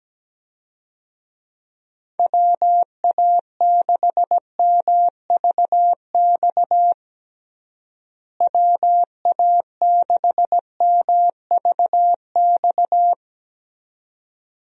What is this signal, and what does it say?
Morse code, heard as a keyed tone, about 17 wpm, 700 Hz. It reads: WA6MVX WA6MVX
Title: USA Call Signs - 6 Characters - Repeated 2x - 17wpm